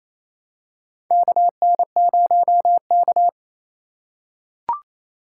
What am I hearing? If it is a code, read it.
Morse code, heard as a keyed tone, about 28 wpm, 700 Hz. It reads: KN0K
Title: USA Call Signs - 4 Characters - 4x Character Spacing - 28wpm